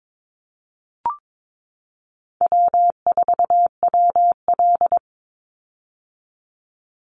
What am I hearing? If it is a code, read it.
Morse code, heard as a keyed tone, about 22 wpm, 700 Hz. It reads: W4WL
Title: USA Call Signs 22wpm